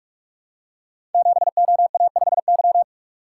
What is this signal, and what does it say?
Morse code, heard as a keyed tone, about 44 wpm, 700 Hz. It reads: 7OA5Y
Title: Code Groups - 5 Letter:Numbers - 45wpm